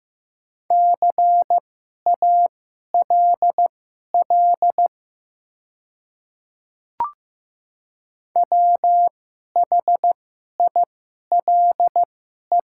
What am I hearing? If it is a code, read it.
Morse code, heard as a keyed tone, about 15 wpm, spaced as for 11 wpm, 700 Hz. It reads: CALL WHILE
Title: 2x Character Spacing - Top 300 Words - 15wpm